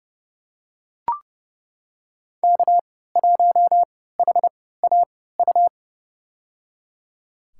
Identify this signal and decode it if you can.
Morse code, heard as a keyed tone, about 30 wpm, spaced as for 17 wpm, 700 Hz. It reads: K1HAU
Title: USA Call Signs - 5 Characters - 3x Character Spacing - 30wpm